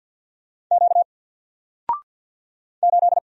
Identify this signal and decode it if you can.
Morse code, heard as a keyed tone, about 50 wpm, 700 Hz. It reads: Q 8
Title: Single Letter-Number 50wpm